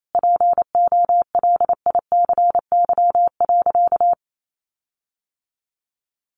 Morse code, heard as a keyed tone, about 28 wpm, 700 Hz. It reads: POLICY.
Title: Sentences from Top 1400 Words 28wpm